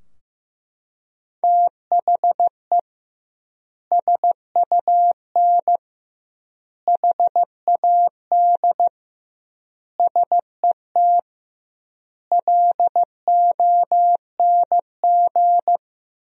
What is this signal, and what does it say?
Morse code, heard as a keyed tone, about 15 wpm, 700 Hz. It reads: THE SUN HAD SET LONG
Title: 2x Word-Spacing - Sentences from Top 300 Words - 15wpm